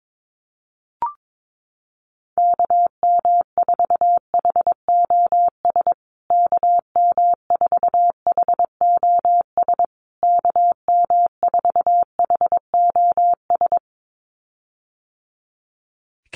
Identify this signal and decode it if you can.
Morse code, heard as a keyed tone, about 22 wpm, 700 Hz. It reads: KM4HOS KM4HOS KM4HOS
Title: USA Call Signs Repeated 3x 22wpm